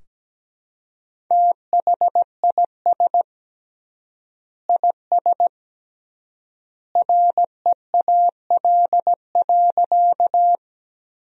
Morse code, heard as a keyed tone, about 17 wpm, 700 Hz. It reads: THIS IS REAL.
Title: 3x Word-Spacing - Sentences from Top 300 Words - 17wpm